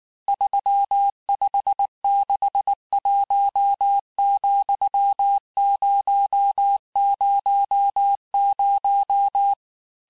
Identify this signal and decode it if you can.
Morse code, heard as a keyed tone, about 19 wpm, 800 Hz. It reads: 3561,000
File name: cours-du-lundi files / cours-1583--texte-chiffre-18-mpm